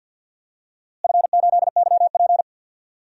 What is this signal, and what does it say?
Morse code, heard as a keyed tone, about 52 wpm, 700 Hz. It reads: F8YP
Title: DXCC-100 Call Signs - 4 Characters - Repeated 2x - 50wpm